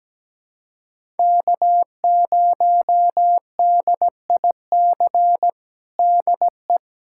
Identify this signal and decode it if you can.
Morse code, heard as a keyed tone, about 17 wpm, 700 Hz. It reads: K0DIC DE
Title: USA Callsign DE Callsign - 5 Characters - 17wpm